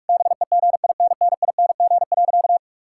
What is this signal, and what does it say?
Morse code, heard as a keyed tone, about 46 wpm, 700 Hz. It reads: BEGINNING.
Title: Sentences from Top 1000 Words - Review Part 5 - 45wpm